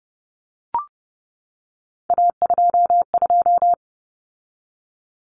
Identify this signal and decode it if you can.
Morse code, heard as a keyed tone, about 30 wpm, 700 Hz. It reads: A22
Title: Common Cut Numbers - 3-Digit 30wpm